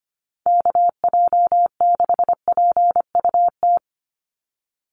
Morse code, heard as a keyed tone, about 25 wpm, 700 Hz. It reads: KJ6PUT
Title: USA Call Signs 25wpm